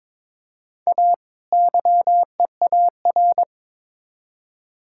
Morse code, heard as a keyed tone, about 22 wpm, 700 Hz. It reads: A YEAR
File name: Build A Phrase - 2-Word Phrases - 22wpm